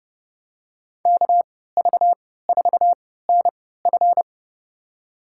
Morse code, heard as a keyed tone, about 30 wpm, spaced as for 17 wpm, 700 Hz. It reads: KV4NF
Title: USA Call Signs - 5 Characters - 3x Character Spacing - 30wpm